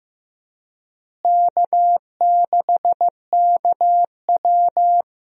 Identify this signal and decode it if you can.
Morse code, heard as a keyed tone, about 15 wpm, 700 Hz. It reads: K6KW
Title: USA Call Signs - 4 Characters - Repeated 2x - 15wpm